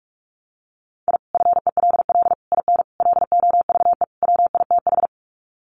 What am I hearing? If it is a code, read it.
Morse code, heard as a keyed tone, about 44 wpm, 700 Hz. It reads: I FELL IN LOVE WITH
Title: Sets of 5 Words Encore 45wpm